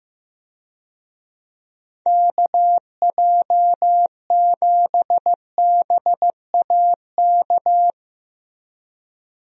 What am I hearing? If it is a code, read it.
Morse code, heard as a keyed tone, about 15 wpm, 700 Hz. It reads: KJ7BAK
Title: USA Call Signs - 6 Characters - Repeated 2x - 15wpm